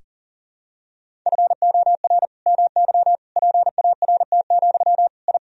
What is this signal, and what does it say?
Morse code, heard as a keyed tone, about 40 wpm, 700 Hz. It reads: FOR MY PART, I
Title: Sentences from Top 1000 Words - Review Part 1 - 40wpm